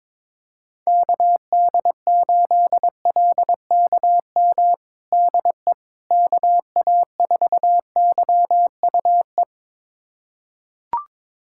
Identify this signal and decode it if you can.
Morse code, heard as a keyed tone, about 21 wpm, 700 Hz. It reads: KD8LKM DE KA4YUE
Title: USA Callsign DE Callsign - 6 Characters - 22wpm